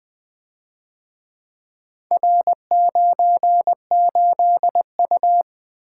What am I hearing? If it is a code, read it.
Morse code, heard as a keyed tone, about 23 wpm, 700 Hz. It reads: R98U
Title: Speed-Racing - Four Letter-Number Combinations - 20wpm